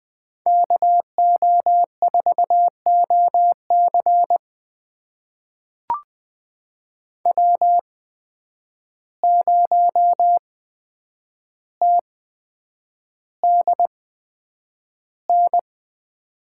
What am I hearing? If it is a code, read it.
Morse code, heard as a keyed tone, about 20 wpm, spaced as for 9 wpm, 700 Hz. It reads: KO4OC W0TDN
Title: USA Call Signs - 5 Characters - 8x Character Spacing - 20wpm